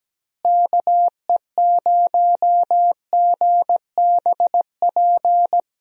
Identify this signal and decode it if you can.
Morse code, heard as a keyed tone, about 17 wpm, 700 Hz. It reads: KE0GBP
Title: USA Call Signs - 6 Characters - Repeated 3x - 17wpm